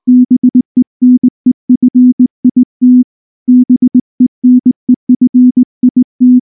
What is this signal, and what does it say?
Morse code, heard as a keyed tone, about 19 wpm, 250 Hz. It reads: BENEFIT BENEFIT
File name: morse / morse-training-20WPM-7